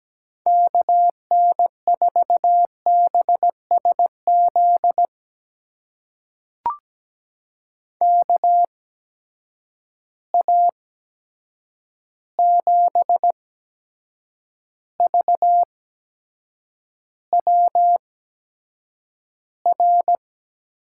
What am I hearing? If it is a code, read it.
Morse code, heard as a keyed tone, about 17 wpm, spaced as for 7 wpm, 700 Hz. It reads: KN4BSZ KA7VWR
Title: USA Call Signs - 6 Characters - 8x Character Spacing - 17wpm